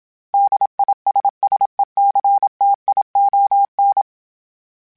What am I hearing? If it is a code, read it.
Morse code, heard as a keyed tone, about 26 wpm, 800 Hz. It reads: DISSECTION